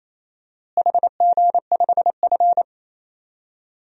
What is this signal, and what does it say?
Morse code, heard as a keyed tone, about 28 wpm, 700 Hz. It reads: HG5F
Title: DXCC-100 Call Signs - 4 Characters - Repeated 3x - 28wpm